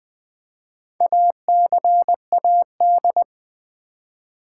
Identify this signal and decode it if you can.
Morse code, heard as a keyed tone, about 20 wpm, 700 Hz. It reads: ACAD
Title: Build A Word - 4 Letters At A Time - 8-Letter Words - 20wpm